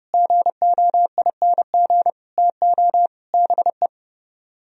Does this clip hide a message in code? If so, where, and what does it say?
Yes: Morse code from 0.1 to 3.9 s, heard as a keyed tone, about 29 wpm, 700 Hz. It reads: GOING TO BE